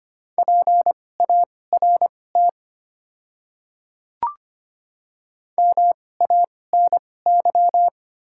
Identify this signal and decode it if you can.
Morse code, heard as a keyed tone, about 25 wpm, spaced as for 18 wpm, 700 Hz. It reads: PART MANY
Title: 2x Character Spacing - Top 200 Words - 25wpm